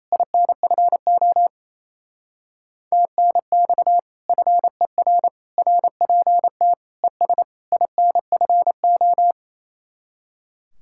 Morse code, heard as a keyed tone, about 34 wpm, 700 Hz. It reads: INFO TNX FER RPT ES INFO
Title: Speed-Racing Common QSO Phrases - 28wpm